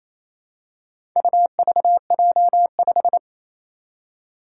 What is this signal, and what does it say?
Morse code, heard as a keyed tone, about 28 wpm, 700 Hz. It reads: UVJ5
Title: Four Challenging Letter-Number Combinations 28wpm